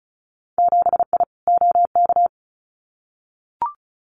Morse code, heard as a keyed tone, about 34 wpm, 700 Hz. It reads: 7I OK
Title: Winter Field Day Exchanges Encore 35wpm